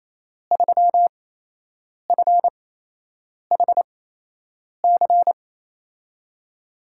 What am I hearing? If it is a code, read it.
Morse code, heard as a keyed tone, about 28 wpm, spaced as for 8 wpm, 700 Hz. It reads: 3FHC
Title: USA Call Signs - 8x Character Spacing - 28wpm